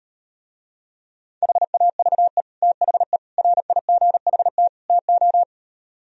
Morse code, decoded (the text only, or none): HAVE THE RIGHT TO